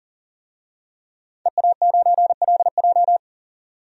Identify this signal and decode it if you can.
Morse code, heard as a keyed tone, about 40 wpm, 700 Hz. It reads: EA9LJ
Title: DXCC-100 Call Signs - 5 Characters - 40wpm